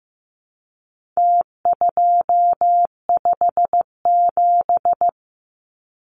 Morse code, heard as a keyed tone, about 14 wpm, 700 Hz. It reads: T257
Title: Numbers - Contest Style - 4-Digit 15wpm